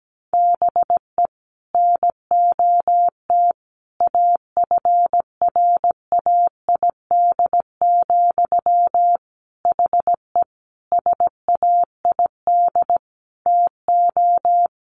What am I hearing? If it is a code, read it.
Morse code, heard as a keyed tone, about 17 wpm, 700 Hz. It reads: BE NOT AFRAID, HE SAID TO
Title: Sentences from Top 1500 Words 17wpm